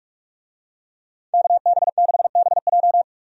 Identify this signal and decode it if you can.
Morse code, heard as a keyed tone, about 46 wpm, 700 Hz. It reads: KB6BJ
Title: USA Call Signs - 5 Characters - 45wpm